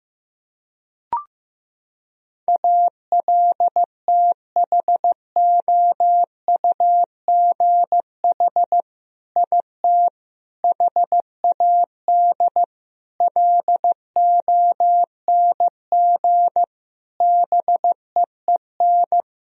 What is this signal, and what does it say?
Morse code, heard as a keyed tone, about 15 wpm, 700 Hz. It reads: ALTHOUGH IT HAD LONG BEEN
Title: Sentences from Top 1000 Words - Review Part 3 - 15wpm